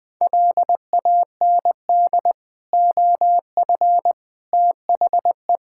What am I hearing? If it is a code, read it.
Morse code, heard as a keyed tone, about 20 wpm, 700 Hz. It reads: LAND OF THE
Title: Sentences with 9-Word Phrases - 20wpm